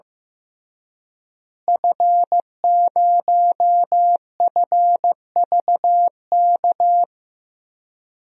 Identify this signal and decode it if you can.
Morse code, heard as a keyed tone, about 15 wpm, 700 Hz. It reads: F0FVK
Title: DXCC-100 Call Signs - 5 Characters - Repeated 3x - 15wpm